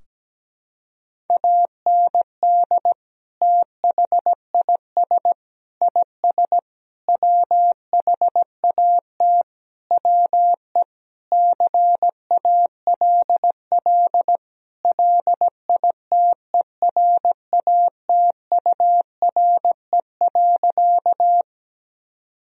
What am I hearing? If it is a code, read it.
Morse code, heard as a keyed tone, about 17 wpm, 700 Hz. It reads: AND THIS IS WHAT WE CALL LITERATURE.
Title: Sentences from Top 1000 Words - Review Part 3 - 17wpm